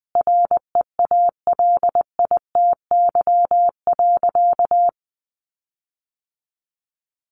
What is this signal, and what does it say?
Morse code, heard as a keyed tone, about 20 wpm, 700 Hz. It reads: REALITY.
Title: Sentences from Top 1500 Words 20wpm